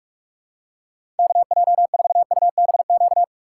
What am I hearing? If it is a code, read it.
Morse code, heard as a keyed tone, about 46 wpm, 700 Hz. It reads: KJ4UBQ
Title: USA Call Signs - 2x Character Spacing - 45wpm